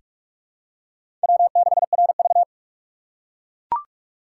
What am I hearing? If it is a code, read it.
Morse code, heard as a keyed tone, about 44 wpm, 700 Hz. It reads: W6RV
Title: USA Call Signs - 4 Characters - 6x Character Spacing - 45wpm